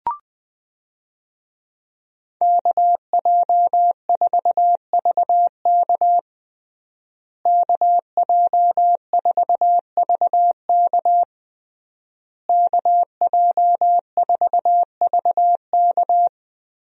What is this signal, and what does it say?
Morse code, heard as a keyed tone, about 20 wpm, 700 Hz. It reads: KJ4VK KJ4VK KJ4VK
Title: USA Call Signs - 5 Characters - Repeated 3x - 20wpm